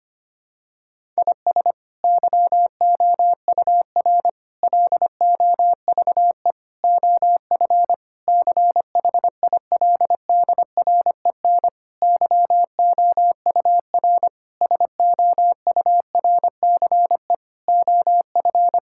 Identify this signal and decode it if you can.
Morse code, heard as a keyed tone, about 25 wpm, 700 Hz. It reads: IS YOUR LOVE OF CHILDREN YOUR SOURCE OF
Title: Sentences from Top 1000 Words - Review Part 5 - 25wpm